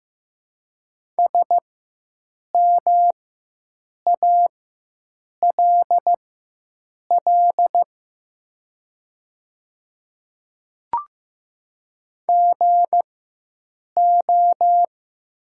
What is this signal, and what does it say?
Morse code, heard as a keyed tone, about 15 wpm, spaced as for 7 wpm, 700 Hz. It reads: SMALL GO
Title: 4x Character Spacing - Top 500 Words - 15wpm